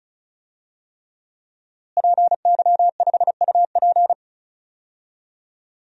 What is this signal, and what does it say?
Morse code, heard as a keyed tone, about 36 wpm, 700 Hz. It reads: PY5UP
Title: DXCC-100 Call Signs - 5 Characters - Repeated 3x - 35wpm